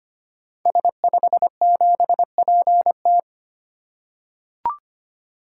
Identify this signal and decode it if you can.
Morse code, heard as a keyed tone, about 25 wpm, 700 Hz. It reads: S57PT
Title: Code Groups - 5 Letter:Numbers - 25wpm